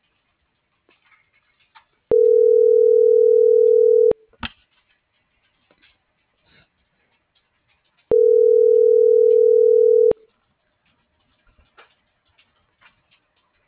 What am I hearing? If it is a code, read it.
Ringback tone (North America), 440 + 480 Hz.